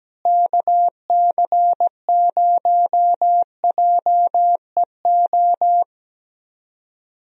Morse code, heard as a keyed tone, about 17 wpm, 700 Hz. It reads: KC0JEO
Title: USA Call Signs - 6 Characters - Repeated 3x - 17wpm